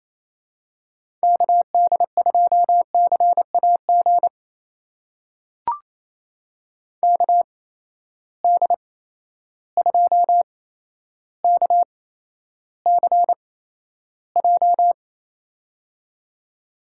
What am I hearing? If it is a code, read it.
Morse code, heard as a keyed tone, about 28 wpm, spaced as for 12 wpm, 700 Hz. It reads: KD2CAG KD2KCJ
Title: USA Call Signs - 6 Characters - 8x Character Spacing - 28wpm